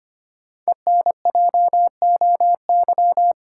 Morse code, heard as a keyed tone, about 25 wpm, 700 Hz. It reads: ENJOY
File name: 5 Letter Words - First 3 Letters Given - 25wpm